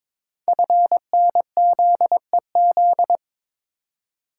Morse code, heard as a keyed tone, about 23 wpm, 700 Hz. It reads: FNZEZ
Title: Code Groups - 5 Letters - 22wpm